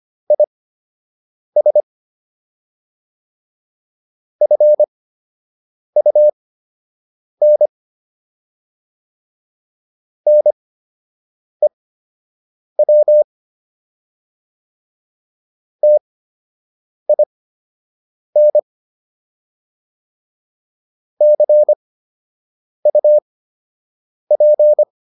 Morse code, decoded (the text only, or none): IS FUN NEW TIN CUP